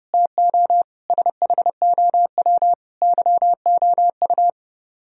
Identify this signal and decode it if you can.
Morse code, heard as a keyed tone, about 36 wpm, 700 Hz. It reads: TO SHOW YOU
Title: Speed-Racing - Sets of 5 Words 30wpm